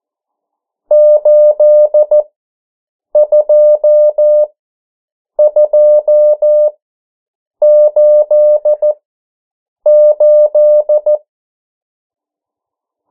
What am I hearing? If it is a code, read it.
Morse code, heard as a keyed tone, about 14 wpm, spaced as for 7 wpm, 600 Hz. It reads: 82288
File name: morse mp3 / 09 urok 8